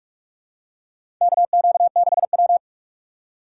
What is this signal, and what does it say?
Morse code, heard as a keyed tone, about 44 wpm, 700 Hz. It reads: KQ6W
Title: USA Call Signs - 3x Character Spacing - 45wpm